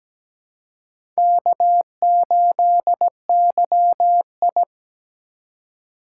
Morse code, heard as a keyed tone, about 17 wpm, 700 Hz. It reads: K8YI
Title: USA Call Signs - 4 Characters - 4x Character Spacing - 17wpm